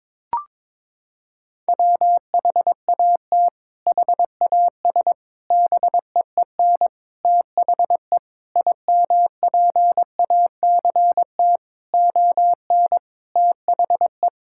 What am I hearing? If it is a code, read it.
Morse code, heard as a keyed tone, about 22 wpm, 700 Hz. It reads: WHAT HAS BEEN THE IMPACT ON THE
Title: Sentences from Top 1000 Words - Review Part 6 - 22wpm